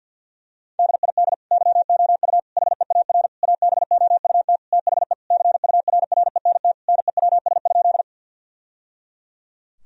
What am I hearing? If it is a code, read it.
Morse code, heard as a keyed tone, about 50 wpm, 700 Hz. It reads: DID YOU HEAR ABOUT THE CURRENT NEWS?
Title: Sentences from Top 1000 Words - Review Part 4 - 50wpm